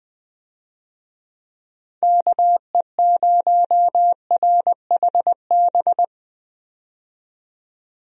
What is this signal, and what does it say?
Morse code, heard as a keyed tone, about 20 wpm, 700 Hz. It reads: KE0RHB
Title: USA Call Signs - 6 Characters - 20wpm